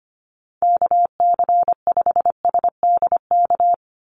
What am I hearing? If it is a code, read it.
Morse code, heard as a keyed tone, about 25 wpm, 690 Hz. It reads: KC5SDK
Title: USA Call Signs 25wpm